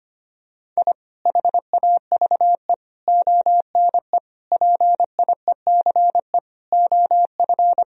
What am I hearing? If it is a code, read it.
Morse code, heard as a keyed tone, about 25 wpm, 700 Hz. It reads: I HAVE ONE PIECE OF
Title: Sentences from Top 1000 Words - Review Part 2 - 25wpm